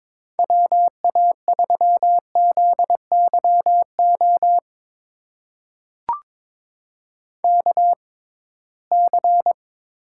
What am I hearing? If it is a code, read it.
Morse code, heard as a keyed tone, about 22 wpm, spaced as for 17 wpm, 700 Hz. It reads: WA3ZYO KC
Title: USA Call Signs - 6 Characters - 6x Character Spacing - 22wpm